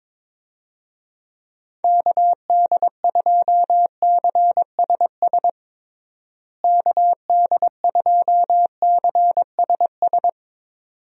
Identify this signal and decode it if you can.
Morse code, heard as a keyed tone, about 22 wpm, 700 Hz. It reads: KD2CSS KD2CSS
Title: USA Call Signs - 6 Characters - Repeated 2x - 22wpm